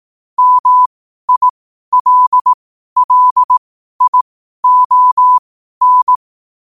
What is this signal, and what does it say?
Morse code, heard as a keyed tone, about 18 wpm, spaced as for 12 wpm, 1000 Hz. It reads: MILLION
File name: qotdfiles / 12wpm